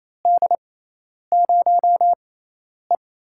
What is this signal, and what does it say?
Morse code, heard as a keyed tone, about 28 wpm, spaced as for 10 wpm, 700 Hz. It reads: D0E
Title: USA Call Signs - 6x Character Spacing - 28wpm